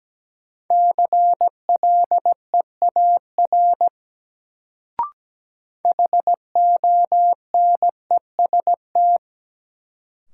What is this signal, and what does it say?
Morse code, heard as a keyed tone, about 17 wpm, 700 Hz. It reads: CLEAR HONEST
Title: Sets of 5 Words Encore - New Word Drill - 17wpm